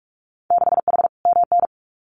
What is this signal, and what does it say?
Morse code, heard as a keyed tone, about 46 wpm, 700 Hz. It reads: 6H MN
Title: Winter Field Day Exchanges Encore 45wpm